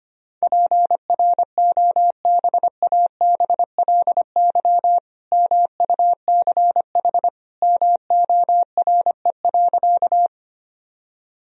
Morse code, heard as a keyed tone, about 25 wpm, 700 Hz. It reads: PROBABLY MUCH MORE.
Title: Sentences from Top 1000 Words - Review Part 4 - 25wpm